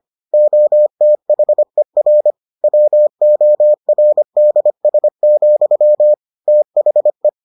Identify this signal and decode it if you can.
Morse code, heard as a keyed tone, about 25 wpm, 600 Hz. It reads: OTHER WORDS, THE